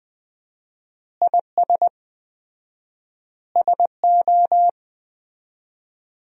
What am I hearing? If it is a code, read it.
Morse code, heard as a keyed tone, about 21 wpm, 700 Hz. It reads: IS SO
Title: 4x Word-Spacing - Sentences from Top 500 Words - 20wpm